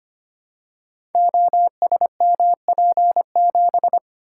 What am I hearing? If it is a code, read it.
Morse code, heard as a keyed tone, about 25 wpm, 700 Hz. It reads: OSMP7
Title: Code Groups - 5 Letter:Numbers - 25wpm